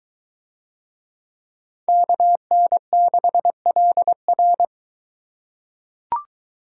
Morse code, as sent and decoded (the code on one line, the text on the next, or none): -.- -. -.... .-.. .-.
KN6LR